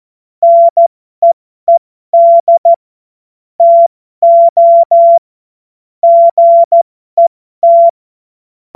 Morse code, heard as a keyed tone, about 14 wpm, spaced as for 12 wpm, 680 Hz. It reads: NEED TO GET
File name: Morsekurs / Witze